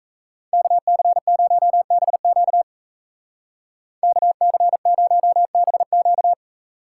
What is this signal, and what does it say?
Morse code, heard as a keyed tone, about 34 wpm, 700 Hz. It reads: KC0BQ KC0BQ
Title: Speed Racing - USA 2x2 Call Signs 28wpm